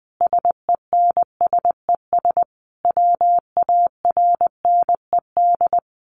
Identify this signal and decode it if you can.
Morse code, heard as a keyed tone, about 20 wpm, 700 Hz. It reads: SENSES WARNED
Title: A Sound Of Thunder 20wpm